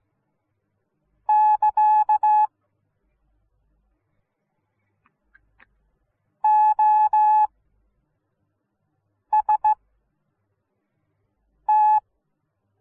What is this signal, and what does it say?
Morse code, heard as a keyed tone, about 15 wpm, spaced as for 4 wpm, 850 Hz. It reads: <KA> OST